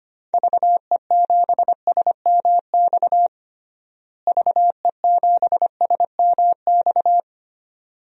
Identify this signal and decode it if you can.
Morse code, heard as a keyed tone, about 25 wpm, 700 Hz. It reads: VE7SMX VE7SMX
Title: DXCC-100 Call Signs - 6 Characters - Repeated 2x - 25wpm